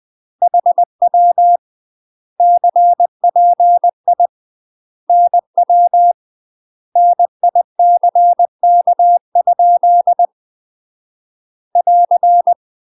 Morse code, heard as a keyed tone, about 20 wpm, 700 Hz. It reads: HW CPI NW NICK? <AR>
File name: UK MASTER QSOs TRAINING / 20WPM / UK MASTER QSOs 0053